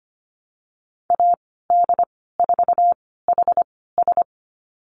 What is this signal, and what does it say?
Morse code, heard as a keyed tone, about 25 wpm, spaced as for 16 wpm, 700 Hz. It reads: AD4HS
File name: Lesson 15g - Number 5 - Call Signs 25wpm